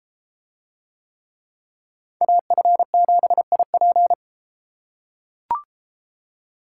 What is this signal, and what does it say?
Morse code, heard as a keyed tone, about 31 wpm, 700 Hz. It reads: AF7IP